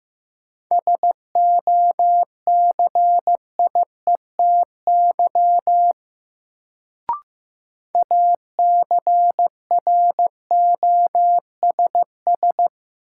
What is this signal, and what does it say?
Morse code, heard as a keyed tone, about 15 wpm, 700 Hz. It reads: SOCIETY ACROSS
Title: Sentences from Top 700 Words - New Word Drill - 15wpm